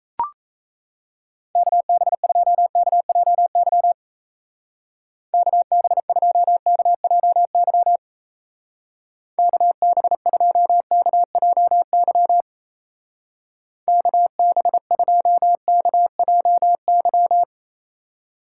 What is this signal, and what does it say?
Morse code, heard as a keyed tone, about 36 wpm, 700 Hz. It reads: KB2KJY KB2KJY KB2KJY KB2KJY